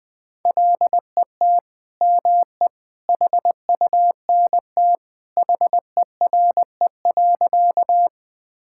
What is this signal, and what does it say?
Morse code, heard as a keyed tone, about 20 wpm, 700 Hz. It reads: LET ME HUNT HERE.
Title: Sentences from Top 1600 Words 20wpm